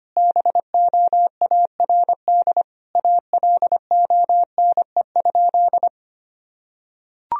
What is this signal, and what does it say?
Morse code, heard as a keyed tone, about 25 wpm, 700 Hz. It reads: BOARD ALONE?
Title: Sentences from Top 1000 Words - Review Part 4 - 25wpm